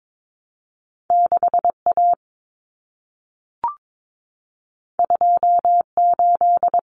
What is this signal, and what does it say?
Morse code, heard as a keyed tone, about 22 wpm, 700 Hz. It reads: 6A 28
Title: Common Cut Numbers - 2-Digit 22wpm